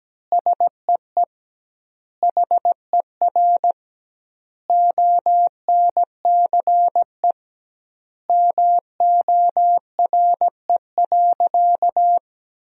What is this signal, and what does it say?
Morse code, heard as a keyed tone, about 17 wpm, 700 Hz. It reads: SEE HER ONCE MORE.
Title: 2x Word-Spacing - Sentences from Top 500 Words - 17wpm